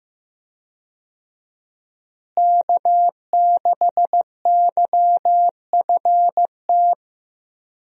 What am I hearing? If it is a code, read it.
Morse code, heard as a keyed tone, about 14 wpm, 700 Hz. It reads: K6YFT
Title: USA Call Signs - 5 Characters - Repeated 3x - 15wpm